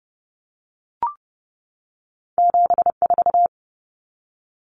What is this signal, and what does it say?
Morse code, heard as a keyed tone, about 30 wpm, 700 Hz. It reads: 74
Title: Common Cut Numbers - 2-Digit 30wpm